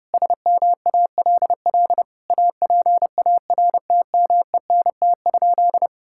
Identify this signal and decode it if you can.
Morse code, heard as a keyed tone, about 30 wpm, 700 Hz. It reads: SMALL APARTMENT?
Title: Sentences from Top 1000 Words - Review Part 6 - 30wpm